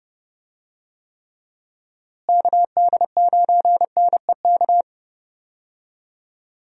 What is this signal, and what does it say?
Morse code, heard as a keyed tone, about 30 wpm, 700 Hz. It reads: KD9NEK
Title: USA Call Signs - 6 Characters - Repeated 3x - 30wpm